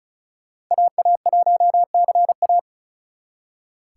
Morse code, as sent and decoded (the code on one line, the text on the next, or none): .- .- .---- -.-. .-
AA1CA